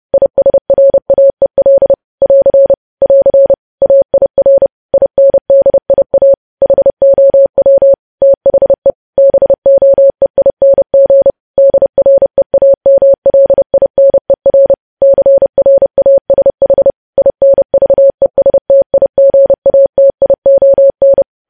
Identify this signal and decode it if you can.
Morse code, heard as a keyed tone, about 30 wpm, 550 Hz. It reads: ISRAEL <AR> <AR> AIR INDIA HOW THE BOEING DREAMLINER CRASH INVESTIGATION
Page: news headlines in morse code at 30 wpm